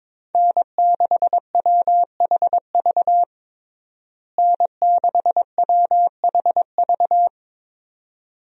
Morse code, heard as a keyed tone, about 22 wpm, 700 Hz. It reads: N6WHV N6WHV
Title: USA Call Signs - 5 Characters - Repeated 2x - 22wpm